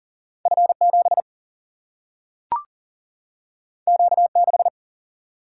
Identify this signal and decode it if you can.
Morse code, heard as a keyed tone, about 40 wpm, 700 Hz. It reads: F7 Q6
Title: Single-Letter Single-Number 40wpm